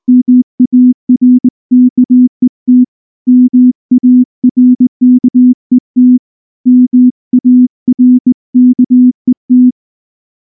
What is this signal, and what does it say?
Morse code, heard as a keyed tone, about 19 wpm, 250 Hz. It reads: MARKET MARKET MARKET
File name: morse / morse-training-20WPM-7